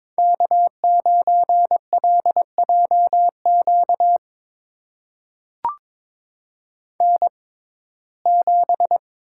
Morse code, heard as a keyed tone, about 23 wpm, spaced as for 16 wpm, 700 Hz. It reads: K9LJQ N7
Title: USA Call Signs - 5 Characters - 6x Character Spacing - 22wpm